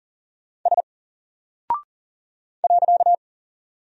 Morse code, heard as a keyed tone, about 40 wpm, 700 Hz. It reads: S .
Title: Single Character (Letter:Number:Punctuation:Prosign) 40wpm